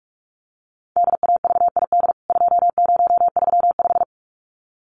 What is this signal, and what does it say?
Morse code, heard as a keyed tone, about 44 wpm, 700 Hz. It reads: DAVID 2035
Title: CWT Exchange 45wpm